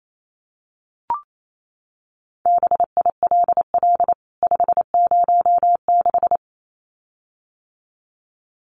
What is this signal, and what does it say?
Morse code, heard as a keyed tone, about 28 wpm, 700 Hz. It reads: BILL 506
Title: CWT Exchange 28wpm